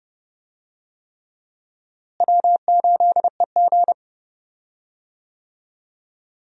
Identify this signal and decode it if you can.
Morse code, heard as a keyed tone, about 31 wpm, 700 Hz. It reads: W8EG
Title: USA Call Signs - 4 Characters - Repeated 3x - 30wpm